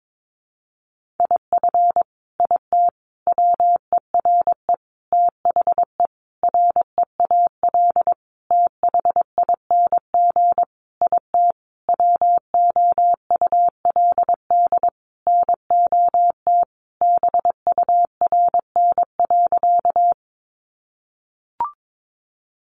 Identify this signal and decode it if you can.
Morse code, heard as a keyed tone, about 22 wpm, 700 Hz. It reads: IF IT WERE THE REAL THING IT WOULD NOT BURN.
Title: Sentences from Top 1300 Words 22wpm